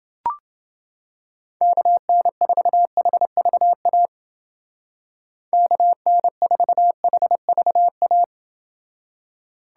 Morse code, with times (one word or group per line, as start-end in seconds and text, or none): 1.6-4.1 KN4HVA
5.5-8.2 KN4HVA